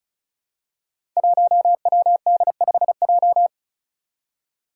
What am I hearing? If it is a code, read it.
Morse code, heard as a keyed tone, about 36 wpm, 700 Hz. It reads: 1WD5J
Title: Five Challenging Letter-Number Combinations 35wpm